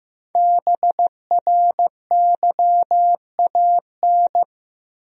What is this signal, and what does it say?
Morse code, heard as a keyed tone, about 15 wpm, 700 Hz. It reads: BRYAN
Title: USA Common Names - Max 5 Letters - 15wpm